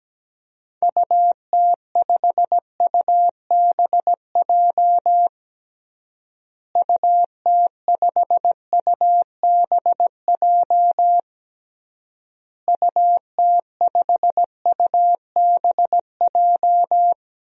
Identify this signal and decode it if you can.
Morse code, heard as a keyed tone, about 17 wpm, 700 Hz. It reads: UT5UBJ UT5UBJ UT5UBJ
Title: DXCC-100 Call Signs - 6 Characters - Repeated 3x - 17wpm